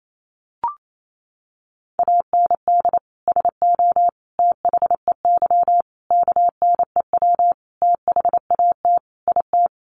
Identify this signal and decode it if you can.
Morse code, heard as a keyed tone, about 28 wpm, 700 Hz. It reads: AND SO THEY KNEW THAT IT
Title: Sentences from Top 400 Words 28wpm